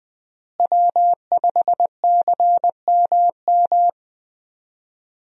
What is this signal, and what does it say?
Morse code, heard as a keyed tone, about 20 wpm, 700 Hz. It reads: W5CMM
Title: USA Call Signs - 5 Characters - Repeated 2x - 20wpm